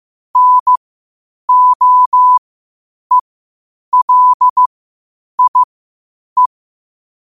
Morse code, heard as a keyed tone, about 15 wpm, spaced as for 8 wpm, 1000 Hz. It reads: NOELIE
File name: qotdfiles / 8wpm